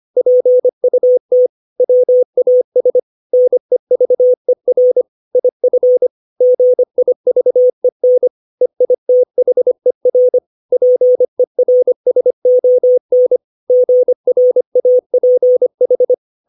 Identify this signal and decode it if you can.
Morse code, heard as a keyed tone, about 25 wpm, 500 Hz. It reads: PUT WAS NEVER IF GIVEN EITHER PERSON GRAPH
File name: 632 words / 632 Words 25WPM